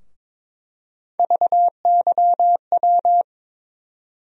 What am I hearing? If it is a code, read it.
Morse code, heard as a keyed tone, about 22 wpm, 700 Hz. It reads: VYW
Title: USA Call Sign Postfixes - 3 Letters 22wpm